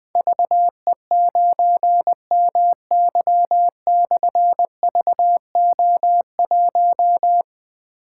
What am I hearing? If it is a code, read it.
Morse code, heard as a keyed tone, about 20 wpm, 700 Hz. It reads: VE9MY/VO1
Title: Difficult Call Signs 20wpm